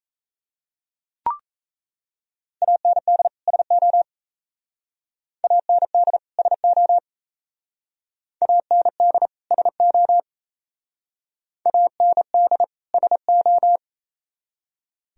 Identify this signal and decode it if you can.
Morse code, heard as a keyed tone, about 35 wpm, 700 Hz. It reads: AND SO AND SO AND SO AND SO
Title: Speed-Racing - Sets of 2 Words 28wpm